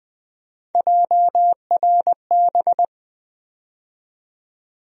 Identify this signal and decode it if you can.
Morse code, heard as a keyed tone, about 20 wpm, 700 Hz. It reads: JRB
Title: USA Call Sign Postfixes - 3 Letters 20wpm